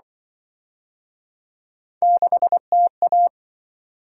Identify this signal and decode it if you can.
Morse code, heard as a keyed tone, about 26 wpm, 700 Hz. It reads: 6TA